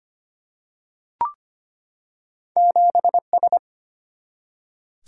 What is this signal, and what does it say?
Morse code, heard as a keyed tone, about 25 wpm, 700 Hz. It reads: 7S
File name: Two Letter-Number Combinations - 25wpm